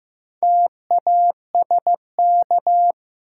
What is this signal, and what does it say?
Morse code, heard as a keyed tone, about 15 wpm, 700 Hz. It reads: TASK
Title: Sentences from Top 1000 Words - Review Part 7 - New Word Drill - 15wpm